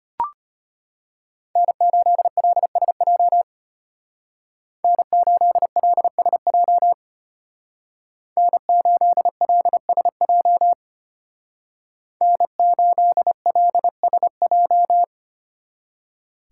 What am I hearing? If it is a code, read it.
Morse code, heard as a keyed tone, about 32 wpm, 700 Hz. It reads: N8LSJ N8LSJ N8LSJ N8LSJ